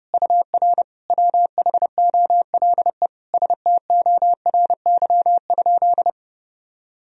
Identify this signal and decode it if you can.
Morse code, heard as a keyed tone, about 30 wpm, 700 Hz. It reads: UR WHOLE STORY?
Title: Sentences from Top 400 Words and Abbreviations 30wpm